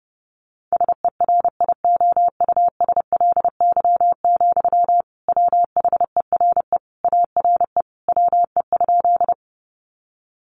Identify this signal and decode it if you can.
Morse code, heard as a keyed tone, about 30 wpm, 700 Hz. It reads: SERIOUSLY, WHERE ARE WE?
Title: Sentences from Top 1200 Words 30wpm